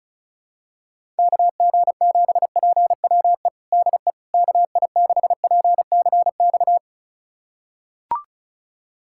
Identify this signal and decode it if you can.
Morse code, heard as a keyed tone, about 36 wpm, 700 Hz. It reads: KG7PWE DE KI6PCX
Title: USA Callsign DE Callsign - 6 Characters - 35wpm